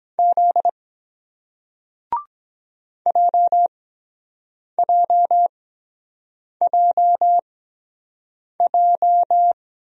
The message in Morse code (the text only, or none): Z JJJJ